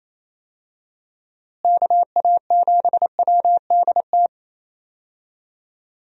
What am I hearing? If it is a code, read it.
Morse code, heard as a keyed tone, about 28 wpm, 700 Hz. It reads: KA7WDT
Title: USA Call Signs - 6 Characters - Repeated 3x - 28wpm